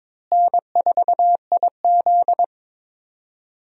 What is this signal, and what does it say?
Morse code, heard as a keyed tone, about 22 wpm, 700 Hz. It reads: N4IZ
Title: USA Call Signs - 4 Characters - Repeated 2x - 22wpm